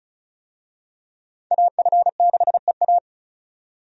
Morse code, heard as a keyed tone, about 34 wpm, 700 Hz. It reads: AF6EA
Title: USA Call Signs - 5 Characters - 35wpm